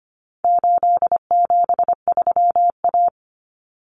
Morse code, heard as a keyed tone, about 25 wpm, 700 Hz. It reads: 873A
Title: Common Cut Numbers - 4-Digit 25wpm